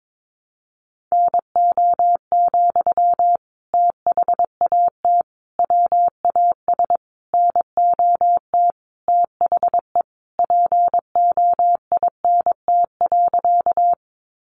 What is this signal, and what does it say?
Morse code, heard as a keyed tone, about 22 wpm, 700 Hz. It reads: NO, THAT WAS NOT THE POINT.
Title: Sentences from Top 600 Words 22wpm